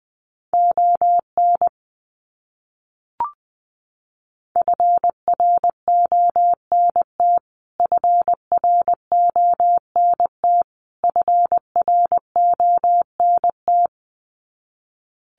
Morse code, heard as a keyed tone, about 20 wpm, 700 Hz. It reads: ON FRONT FRONT FRONT
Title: Top 500 Words Repeated 20wpm